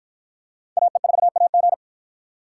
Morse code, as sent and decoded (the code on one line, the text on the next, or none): .- . ....- .- --.
AE4AG